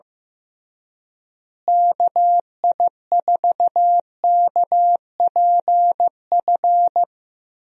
Morse code, heard as a keyed tone, about 15 wpm, 700 Hz. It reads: KI4KPF
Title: USA Call Signs - 6 Characters - Repeated 3x - 15wpm